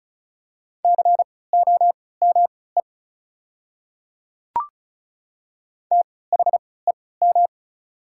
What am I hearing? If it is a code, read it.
Morse code, heard as a keyed tone, about 36 wpm, spaced as for 20 wpm, 690 Hz. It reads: COME THEM
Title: 3x Character Spacing - Top 100 Words - 35wpm